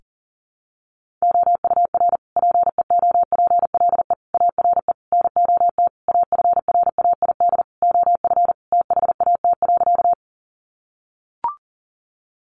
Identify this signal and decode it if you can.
Morse code, heard as a keyed tone, about 40 wpm, 700 Hz. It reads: OUR PEOPLE ARE NOT AFRAID OF THAT.
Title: Sentences from Top 1500 Words 40wpm